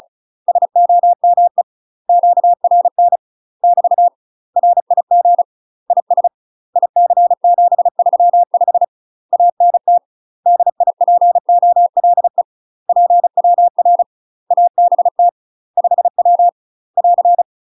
Morse code, heard as a keyed tone, about 36 wpm, 700 Hz. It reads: SOME QRN <BT> RIG IS IC735 ANT DIPOLE PWR ABT 5W <AR>
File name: UK MASTER QSOs TRAINING / 35WPM / UK MASTER QSOs 0024